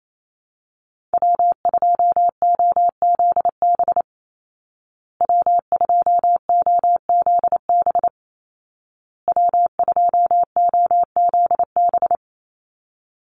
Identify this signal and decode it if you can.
Morse code, heard as a keyed tone, about 28 wpm, 700 Hz. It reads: W2OZB W2OZB W2OZB
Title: Call Signs Repeated 2x - 28wpm